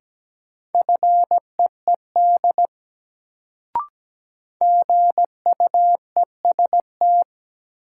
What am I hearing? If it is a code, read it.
Morse code, heard as a keyed tone, about 17 wpm, 700 Hz. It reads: FEED GUEST
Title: Sentences from Top 1600 Words - New Word Drill Alt - 17wpm